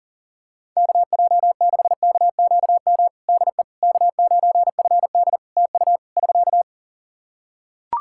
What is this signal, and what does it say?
Morse code, heard as a keyed tone, about 40 wpm, 700 Hz. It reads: KJ6KQM DE K9FD TU <SK>
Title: K1USN SST QSO Practice 40wpm